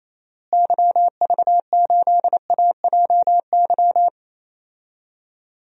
Morse code, heard as a keyed tone, about 28 wpm, 700 Hz. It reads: YV8AJY
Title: DXCC-100 Call Signs - 6 Characters - Repeated 3x - 28wpm